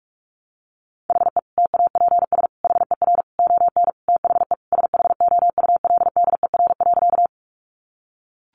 Morse code, heard as a keyed tone, about 44 wpm, 700 Hz. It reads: HE TAPS HER ON THE SHOULDER.
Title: Additional Easy Phrases 45wpm